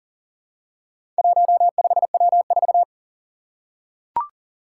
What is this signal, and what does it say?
Morse code, heard as a keyed tone, about 40 wpm, 700 Hz. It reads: 15W4
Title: Four Challenging Letter-Number Combinations 40wpm